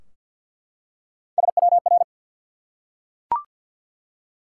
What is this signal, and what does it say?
Morse code, heard as a keyed tone, about 48 wpm, 700 Hz. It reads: SWR
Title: Top 180 QSO Elements 50wpm